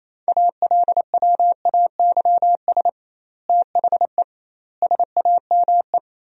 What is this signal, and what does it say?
Morse code, heard as a keyed tone, about 28 wpm, 700 Hz. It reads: ALWAYS THE SAME
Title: 2x Word-Spacing - Sentences from Top 300 Words - 28wpm